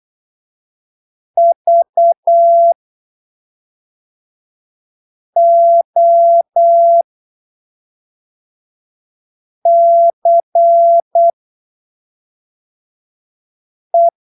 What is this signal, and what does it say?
Morse code, heard as a keyed tone, about 8 wpm, spaced as for 3 wpm, 680 Hz. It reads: VOCE
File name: arquivos / texto / texto 16 08ppm